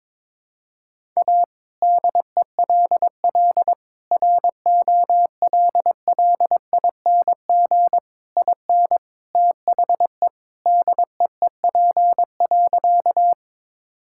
Morse code, heard as a keyed tone, about 22 wpm, 700 Hz. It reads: A DELL ROLLING IN THE DEEP.
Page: morse-code-ninja-practice-22wpm